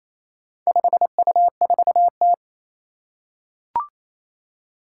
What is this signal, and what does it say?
Morse code, heard as a keyed tone, about 28 wpm, 700 Hz. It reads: 5U4T
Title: Four Challenging Letter-Number Combinations 28wpm